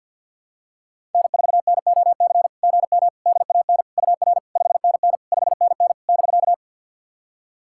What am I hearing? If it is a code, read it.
Morse code, heard as a keyed tone, about 53 wpm, 690 Hz. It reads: N4NOC GM DAN UR 5NN 5NN <BK>
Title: SOTA QSO Practice 50wpm